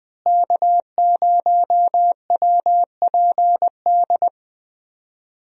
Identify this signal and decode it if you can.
Morse code, heard as a keyed tone, about 21 wpm, 700 Hz. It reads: K0WPD
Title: USA Call Signs - 5 Characters - Repeated 2x - 20wpm